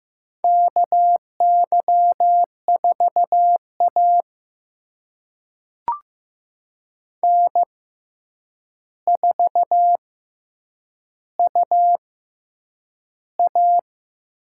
Morse code, heard as a keyed tone, about 15 wpm, spaced as for 8 wpm, 700 Hz. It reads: KY4A N4UA
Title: USA Call Signs - 4 Characters - 6x Character Spacing - 15wpm